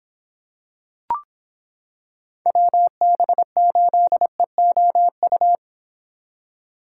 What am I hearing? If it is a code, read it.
Morse code, heard as a keyed tone, about 26 wpm, 700 Hz. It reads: WB8EOU